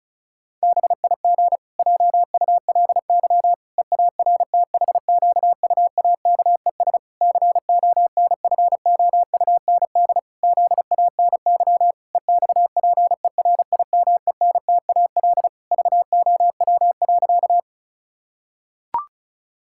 Morse code, heard as a keyed tone, about 34 wpm, 700 Hz. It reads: BIG JULY EARTHQUAKES CONFOUND ZANY EXPERIMENTAL VOW.
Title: Pangram Challenge v2 - 35wpm